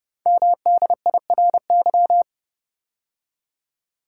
Morse code, heard as a keyed tone, about 30 wpm, 700 Hz. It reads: MDIRY
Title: Code Groups - 5 Letters - 30wpm